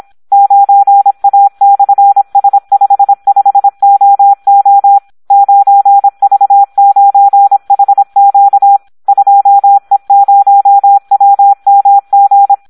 Morse code, heard as a keyed tone, about 26 wpm, 800 Hz. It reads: 9A/S55OO 9V9HQ 2E0WMG
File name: Audio practica CW / SD 1 26